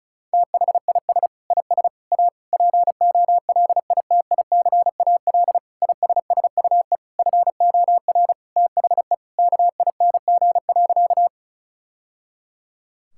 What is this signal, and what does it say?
Morse code, heard as a keyed tone, about 34 wpm, 700 Hz. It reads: THIS IS A POLITICAL ISSUE FOR THE KING.
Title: Sentences from Top 1000 Words - Review Part 6 - 35wpm